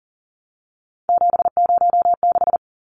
Morse code, heard as a keyed tone, about 40 wpm, 700 Hz. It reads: 706
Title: Numbers - 3-Digit 40wpm